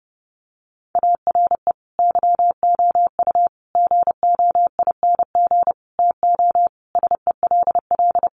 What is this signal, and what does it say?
Morse code, heard as a keyed tone, about 30 wpm, 700 Hz. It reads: ARE YOU GOING TO SELL
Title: Sentences from Top 1200 Words 30wpm